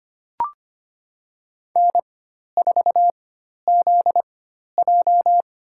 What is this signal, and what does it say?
Morse code, heard as a keyed tone, about 25 wpm, spaced as for 12 wpm, 700 Hz. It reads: N4ZJ
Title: USA Call Signs - 4 Characters - 4x Character Spacing - 25wpm